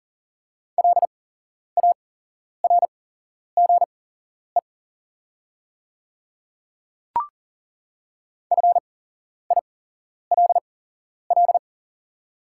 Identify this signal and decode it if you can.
Morse code, heard as a keyed tone, about 40 wpm, spaced as for 11 wpm, 700 Hz. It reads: LARGE FILL
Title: 8x Character Spacing - Top 500 Words - 40wpm